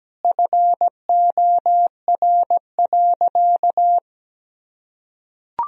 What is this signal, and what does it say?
Morse code, heard as a keyed tone, about 17 wpm, 700 Hz. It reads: FOR.
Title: Sentences from Top 1000 Words - Review Part 3 - 17wpm